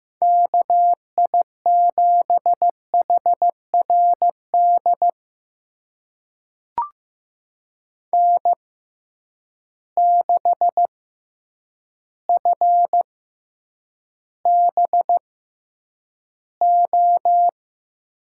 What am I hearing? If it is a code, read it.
Morse code, heard as a keyed tone, about 15 wpm, spaced as for 8 wpm, 700 Hz. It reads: KI7HRD N6FBO
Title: USA Call Signs - 6x Character Spacing - 15wpm